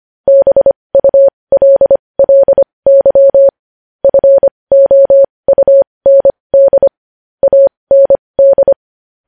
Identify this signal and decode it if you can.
Morse code, heard as a keyed tone, about 25 wpm, spaced as for 20 wpm, 550 Hz. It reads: BULLY FOUND AND
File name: podcast / News headlines-1x-25x20